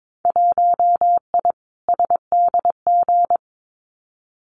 Morse code, heard as a keyed tone, about 22 wpm, 700 Hz. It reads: 1I SDG
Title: Winter Field Day Exchanges 22wpm